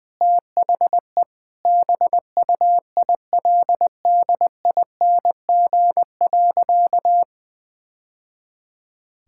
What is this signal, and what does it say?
Morse code, heard as a keyed tone, about 21 wpm, 700 Hz. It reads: THE BUILDING.
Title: Sentences with 3-Word Phrases - 20wpm